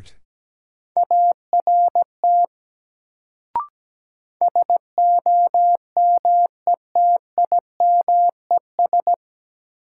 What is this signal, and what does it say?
Morse code, heard as a keyed tone, about 17 wpm, 700 Hz. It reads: ART SOMETIMES
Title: Sentences from Top 600 Words - New Word Drill - 17wpm